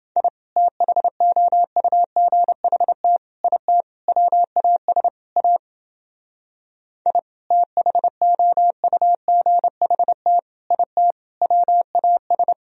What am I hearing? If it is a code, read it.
Morse code, heard as a keyed tone, about 29 wpm, 700 Hz. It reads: I THOUGHT IT WAS A I THOUGHT IT WAS